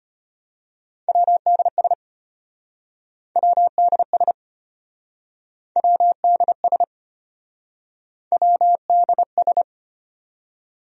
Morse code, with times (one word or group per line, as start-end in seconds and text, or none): 1.1-1.9 WDS
3.4-4.3 WDS
5.8-6.8 WDS
8.3-9.6 WDS